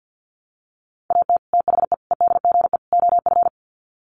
Morse code, heard as a keyed tone, about 50 wpm, 700 Hz. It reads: AT THE EDGE OF
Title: Sets of 4 Words 50wpm